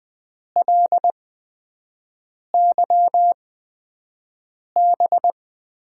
Morse code, heard as a keyed tone, about 20 wpm, spaced as for 5 wpm, 700 Hz. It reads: LYB
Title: USA Call Signs - 6 Characters - 8x Character Spacing - 20wpm